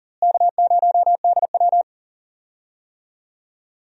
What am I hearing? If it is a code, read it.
Morse code, heard as a keyed tone, about 40 wpm, 700 Hz. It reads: K0DW
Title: USA Call Signs - 4 Characters - Repeated 3x - 40wpm